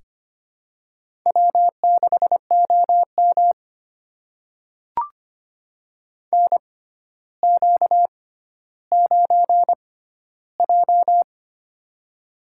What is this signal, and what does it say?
Morse code, heard as a keyed tone, about 25 wpm, spaced as for 13 wpm, 700 Hz. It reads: W6OM NQ9J
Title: USA Call Signs - 4 Characters - 6x Character Spacing - 25wpm